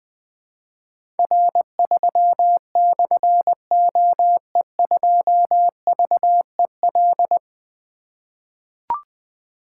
Morse code, heard as a keyed tone, about 20 wpm, 700 Hz. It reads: R3/OE2VEL
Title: Difficult Call Signs 20wpm